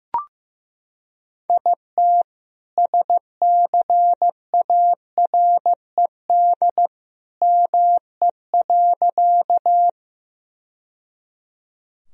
Morse code, heard as a keyed tone, about 15 wpm, 700 Hz. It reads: IT SCARED ME.
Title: Sentences from Top 1000 Words - Review Part 4 - 15wpm